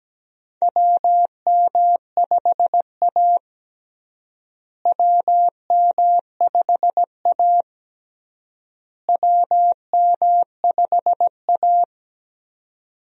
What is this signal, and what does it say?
Morse code, heard as a keyed tone, about 17 wpm, 700 Hz. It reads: WM5A WM5A WM5A
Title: USA Call Signs - 4 Characters - Repeated 3x - 17wpm